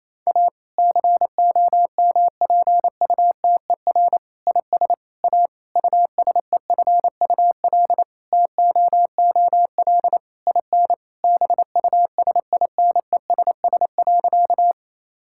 Morse code, heard as a keyed tone, about 28 wpm, 700 Hz. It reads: A COMPUTER IS A USEFUL TOOL IN BUSINESS.
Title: Sentences from Top 1000 Words - Review Part 6 - 28wpm